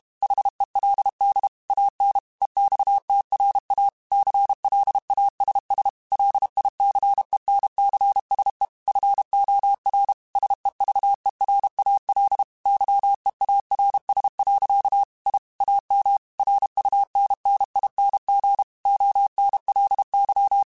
Morse code, heard as a keyed tone, about 32 wpm, 800 Hz. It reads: HELD AN EXTRA CLASS LICENCSE FOR SEVERAL YEARS. I AM RUNNING ONLY
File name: QSOs / QSOs-mp3s / kc7sac@30